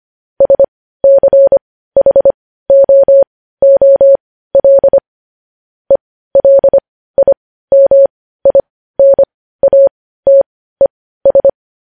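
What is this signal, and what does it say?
Morse code, heard as a keyed tone, about 25 wpm, spaced as for 15 wpm, 550 Hz. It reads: SCHOOL ELIMINATES
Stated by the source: Morse code news headlines sent at 15, preceded by vvv.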